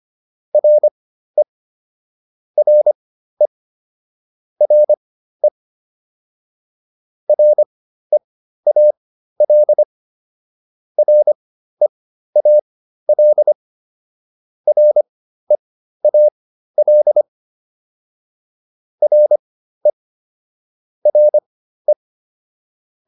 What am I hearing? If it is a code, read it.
Morse code, heard as a keyed tone, about 25 wpm, spaced as for 13 wpm, 600 Hz. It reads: RE RE RE REAL REAL REAL RE RE